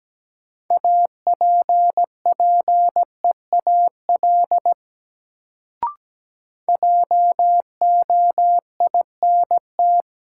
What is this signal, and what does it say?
Morse code, heard as a keyed tone, about 17 wpm, 700 Hz. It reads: APPEAL JOINT
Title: Sentences from Top 1300 Words - New Word Drill - 17wpm